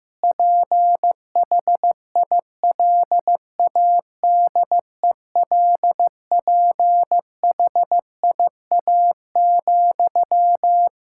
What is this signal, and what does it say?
Morse code, heard as a keyed tone, about 15 wpm, 700 Hz. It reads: PHILADELPHIA,
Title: Practice QSO - Exchange 1 - Beginning - 15wpm